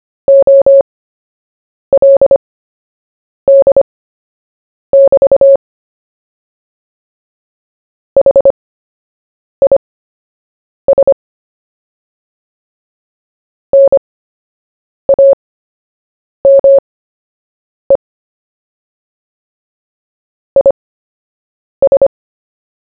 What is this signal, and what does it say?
Morse code, heard as a keyed tone, about 25 wpm, spaced as for 7 wpm, 550 Hz. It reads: OLD<BT> HIS NAME IS
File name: cw / fists / Short-Story-1-7wpm